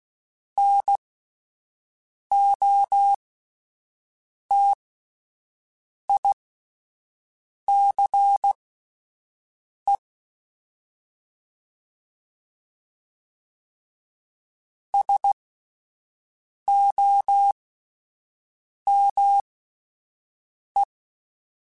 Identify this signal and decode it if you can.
Morse code, heard as a keyed tone, about 16 wpm, spaced as for 5 wpm, 800 Hz. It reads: NOTICE SOME
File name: QSOs / QSOs-mp3s / WD0BGP 1@13